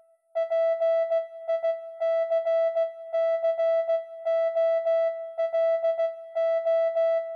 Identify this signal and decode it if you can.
Morse code, heard as a keyed tone, about 16 wpm, 660 Hz. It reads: PICCOLO